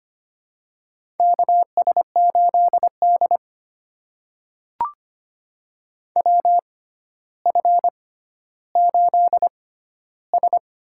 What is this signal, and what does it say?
Morse code, heard as a keyed tone, about 25 wpm, spaced as for 13 wpm, 700 Hz. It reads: KS8D WF8S
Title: USA Call Signs - 4 Characters - 6x Character Spacing - 25wpm